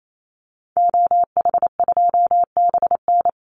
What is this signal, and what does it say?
Morse code, heard as a keyed tone, about 28 wpm, 700 Hz. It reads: OH2BN
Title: Call Signs Repeated 2x - 28wpm